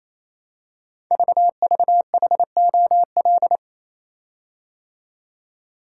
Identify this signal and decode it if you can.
Morse code, heard as a keyed tone, about 27 wpm, 700 Hz. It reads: VVHOL
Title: Code Groups - 5 Letters - 28wpm